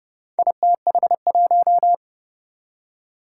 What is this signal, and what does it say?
Morse code, heard as a keyed tone, about 30 wpm, 700 Hz. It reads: ITH1
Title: Four Challenging Letter-Number Combinations 30wpm